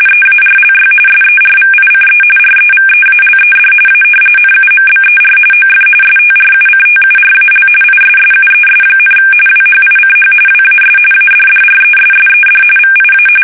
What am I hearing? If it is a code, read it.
RTTY, about 45.45 baud. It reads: EARLY SERVICE REVEALED A NUMBER OF STRUCTURAL ISSUES, WHICH WAS ADDRESSED VIA
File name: fsk850 / battleships